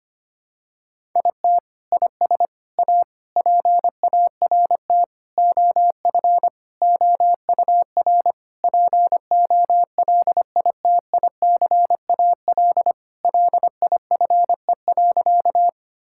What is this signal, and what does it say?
Morse code, heard as a keyed tone, about 25 wpm, 700 Hz. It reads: IT IS A PART OF OUR POLITICAL LIFE.
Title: Sentences from Top 1000 Words - Review Part 5 - 25wpm